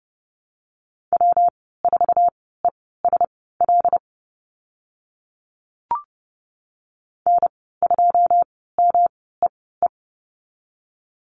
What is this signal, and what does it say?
Morse code, heard as a keyed tone, about 30 wpm, spaced as for 17 wpm, 700 Hz. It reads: W4ESL N2MEE
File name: Lesson 19g - Letter w - Call Signs 30wpm